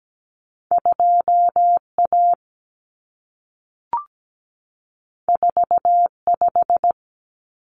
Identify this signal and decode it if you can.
Morse code, heard as a keyed tone, about 17 wpm, 700 Hz. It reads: 2A 45
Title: Common Cut Numbers - 2-Digit 17wpm